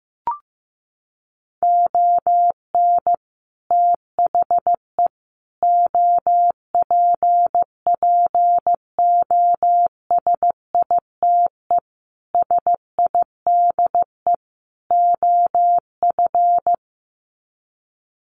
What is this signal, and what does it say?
Morse code, heard as a keyed tone, about 15 wpm, 700 Hz. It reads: ON THE OPPOSITE SIDE OF
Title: Sets of 5 Words Encore2 15wpm